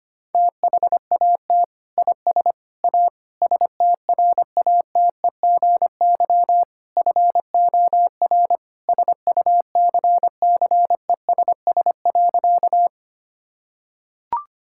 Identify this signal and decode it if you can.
Morse code, heard as a keyed tone, about 25 wpm, 700 Hz. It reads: THAT IS A STRATEGY FOR SUCCESS.
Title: Sentences from Top 1000 Words - Review Part 5 - 25wpm